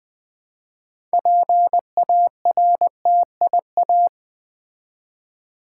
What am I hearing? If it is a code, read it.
Morse code, heard as a keyed tone, about 20 wpm, 700 Hz. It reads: PARTIA
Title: Build A Word - 3 Letters At A Time - 9-Letter Words - 20wpm